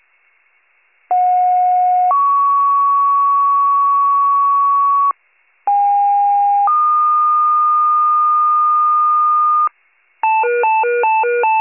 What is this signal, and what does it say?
Two-tone page, heard as a one-second tone, then a three-second tone, about 730 then 1120 Hz, about 810 then 1220 Hz. It is heard from 1.1 to 9.7 s.